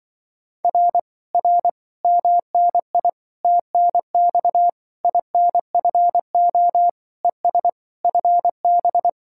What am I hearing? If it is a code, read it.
Morse code, heard as a keyed tone, about 24 wpm, 700 Hz. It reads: R R MNI TNX INFO ES FB
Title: Speed-Racing Common QSO Phrases - 20wpm